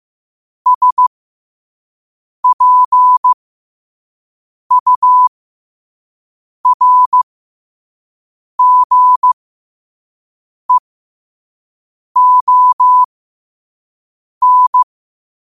Morse code, heard as a keyed tone, about 15 wpm, spaced as for 5 wpm, 1000 Hz. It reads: SPURGEON
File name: qotdfiles / 5wpm